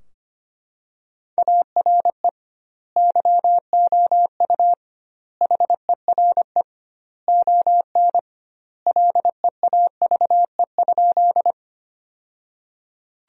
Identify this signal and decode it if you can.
Morse code, heard as a keyed tone, about 25 wpm, 700 Hz. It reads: ARE YOU HERE ON LEAVE?
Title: 2x Word-Spacing - Sentences from Top 400 Words - 25wpm